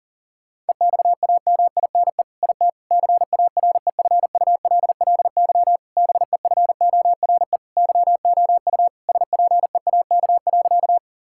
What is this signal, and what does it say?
Morse code, heard as a keyed tone, about 40 wpm, 700 Hz. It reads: EXAMINE IT CAREFULLY BEFORE YOU SPEAK.
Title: Sentences from Top 1600 Words 40wpm